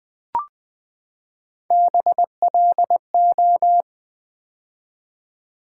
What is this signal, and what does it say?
Morse code, heard as a keyed tone, about 20 wpm, 690 Hz. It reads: BLO
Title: Build A Word - 3 Letters At A Time - 6-Letter Words - 20wpm